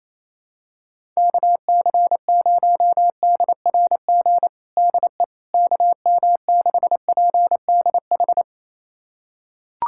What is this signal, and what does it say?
Morse code, heard as a keyed tone, about 28 wpm, 700 Hz. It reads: KC0DRG DE KM6PDH
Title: USA Callsign DE Callsign - 6 Characters - 28wpm